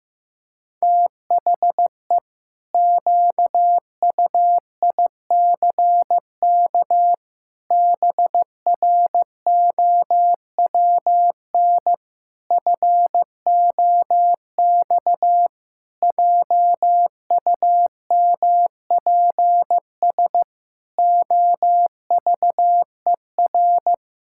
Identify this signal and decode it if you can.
Morse code, heard as a keyed tone, about 15 wpm, 700 Hz. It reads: THE QUICK BROWN FOX JUMPS OVER
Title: Pangram Challenge v2 - 15wpm